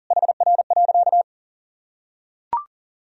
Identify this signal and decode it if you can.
Morse code, heard as a keyed tone, about 40 wpm, 700 Hz. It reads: HR.
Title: Sentences from Top 300 Words and Abbreviations 40wpm